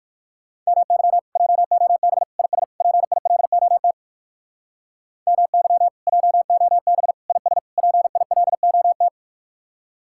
Morse code, heard as a keyed tone, about 63 wpm, 700 Hz. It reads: MY JOB IS PILOT MY JOB IS PILOT